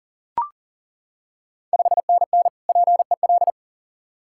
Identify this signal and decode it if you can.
Morse code, heard as a keyed tone, about 40 wpm, 690 Hz. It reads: 5NN PEL
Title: Ontario QSO Party Exchange 40wpm